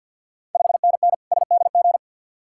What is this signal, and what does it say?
Morse code, heard as a keyed tone, about 52 wpm, 700 Hz. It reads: 5NN SDG
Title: Ontario QSO Party Exchange 50wpm